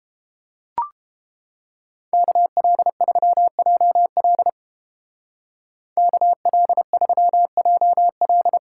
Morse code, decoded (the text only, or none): KL3JL KL3JL